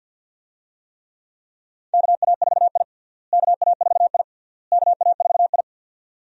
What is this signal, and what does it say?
Morse code, heard as a keyed tone, about 52 wpm, 700 Hz. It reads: KA4I KA4I KA4I
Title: USA Call Signs - 4 Characters - Repeated 3x - 50wpm